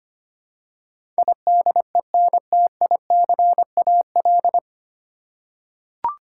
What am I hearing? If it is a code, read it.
Morse code, heard as a keyed tone, about 25 wpm, 700 Hz. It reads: IDENTICAL
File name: Nine Letter Words - 25wpm